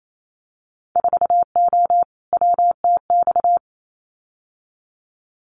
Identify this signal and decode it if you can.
Morse code, heard as a keyed tone, about 28 wpm, 700 Hz. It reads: 4O WTX
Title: Winter Field Day Exchanges Encore 28wpm